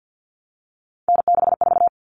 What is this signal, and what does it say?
Morse code, heard as a keyed tone, about 50 wpm, 700 Hz. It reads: N64
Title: Common Cut Numbers - 3-Digit 50wpm